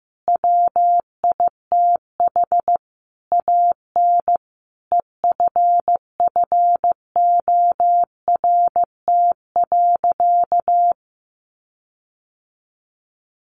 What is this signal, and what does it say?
Morse code, heard as a keyed tone, about 15 wpm, 700 Hz. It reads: WITH AN EFFORT.
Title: Sentences from Top 1500 Words 15wpm